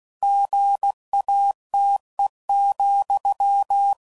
Morse code, heard as a keyed tone, about 16 wpm, 800 Hz. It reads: GATE,
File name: CW Code Groups @15WPM / Technical Subjects / OR Gate